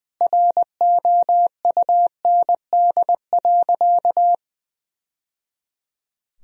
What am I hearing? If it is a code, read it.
Morse code, heard as a keyed tone, about 20 wpm, 700 Hz. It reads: ROUND.
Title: Sentences from Top 1000 Words - Review Part 1 - 20wpm